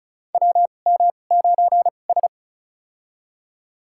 Morse code, heard as a keyed tone, about 36 wpm, spaced as for 25 wpm, 700 Hz. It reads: WM9S
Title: USA Call Signs - 4 Characters - 2x Character Spacing - 35wpm